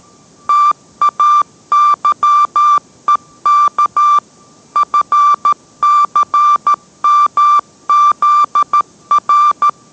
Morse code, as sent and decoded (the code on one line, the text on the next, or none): - .- -.-- . -.- / ..-. -.-. -- --.. .-.
TAYEK FCMZR